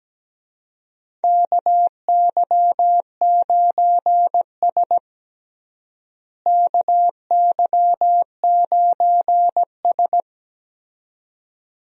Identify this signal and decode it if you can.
Morse code, heard as a keyed tone, about 17 wpm, 700 Hz. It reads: KY9S KY9S
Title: USA Call Signs - 4 Characters - Repeated 2x - 17wpm